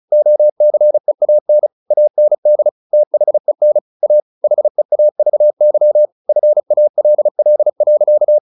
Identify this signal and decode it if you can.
Morse code, heard as a keyed tone, about 35 wpm, 600 Hz. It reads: OCEAN AND THEN A HEAVY FALL.
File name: the old man and the sea 2 35wpm